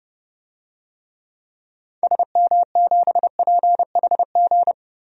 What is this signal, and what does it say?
Morse code, heard as a keyed tone, about 31 wpm, 700 Hz. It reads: SM7PHG
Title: DXCC-100 Call Signs - 6 Characters - 30wpm